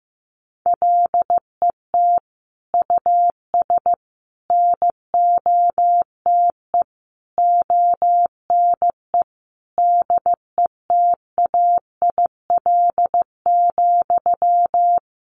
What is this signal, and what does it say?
Morse code, heard as a keyed tone, about 15 wpm, 700 Hz. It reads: LET US NOTE ONE DETAIL,
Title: Sentences from Top 1500 Words 15wpm